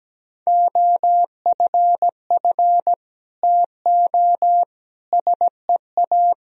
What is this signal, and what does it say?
Morse code, heard as a keyed tone, about 17 wpm, 700 Hz. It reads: OFF TO SEA
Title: Sentences from Top 1000 Words - Review Part 1 - 17wpm